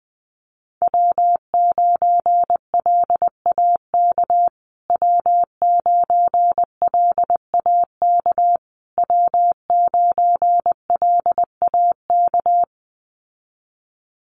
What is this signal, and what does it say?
Morse code, heard as a keyed tone, about 20 wpm, 700 Hz. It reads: W9LAK W9LAK W9LAK
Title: USA Call Signs Repeated 3x 20wpm